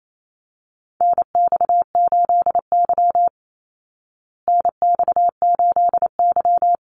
Morse code, heard as a keyed tone, about 28 wpm, 700 Hz. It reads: NX8Y NX8Y
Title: Call Signs Repeated 3x - 28wpm